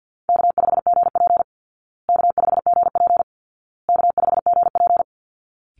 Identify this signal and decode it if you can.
Morse code, heard as a keyed tone, about 50 wpm, 700 Hz. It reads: K5GP K5GP K5GP
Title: Easy Call Signs Repeated 3x - 50wpm